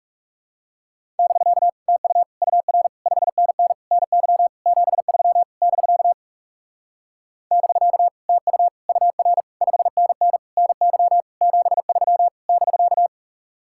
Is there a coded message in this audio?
Yes: Morse code from 1.2 to 13.1 s, heard as a keyed tone, about 40 wpm, 700 Hz. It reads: <BK> TU UR 5NN NY 73 <BK> <BK> TU UR 5NN NY 73 <BK>